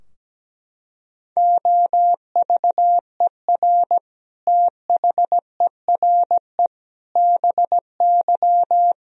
Morse code, heard as a keyed tone, about 17 wpm, 700 Hz. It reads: OVER THERE BY
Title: Sentences from Top 1000 Words - Review Part 3 - 17wpm